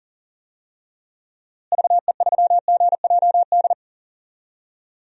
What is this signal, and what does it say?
Morse code, heard as a keyed tone, about 40 wpm, 700 Hz. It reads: VE3GJD
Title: DXCC-100 Call Signs - 6 Characters - Repeated 2x - 40wpm